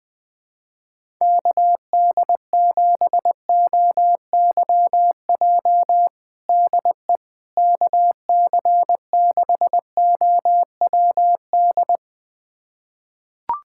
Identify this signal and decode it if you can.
Morse code, heard as a keyed tone, about 21 wpm, 700 Hz. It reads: KD7OYJ DE KC6OWD
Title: USA Callsign DE Callsign - 6 Characters - 20wpm